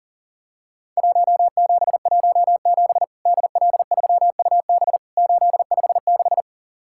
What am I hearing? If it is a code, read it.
Morse code, heard as a keyed tone, about 38 wpm, 700 Hz. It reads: 1717 DL3UB 856
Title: Worked All Europe DX Contest - QTC Exchange - 40wpm